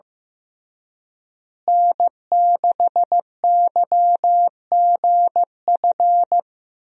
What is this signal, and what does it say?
Morse code, heard as a keyed tone, about 15 wpm, 700 Hz. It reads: N6YGF
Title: USA Call Signs - 5 Characters - Repeated 3x - 15wpm